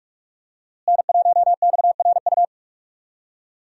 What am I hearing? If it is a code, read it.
Morse code, heard as a keyed tone, about 47 wpm, 700 Hz. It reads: N1XRU
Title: USA Call Signs - 5 Characters - Repeated 2x - 45wpm